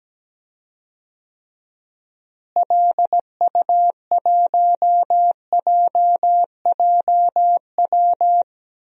Morse code, heard as a keyed tone, about 17 wpm, 700 Hz. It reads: LU1JJW
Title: DXCC-100 Call Signs - 6 Characters - Repeated 3x - 17wpm